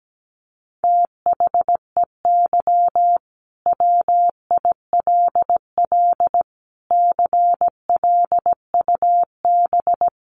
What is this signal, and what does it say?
Morse code, heard as a keyed tone, about 17 wpm, 700 Hz. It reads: THEY WILL CLUB
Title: Sentences from Top 1200 Words 17wpm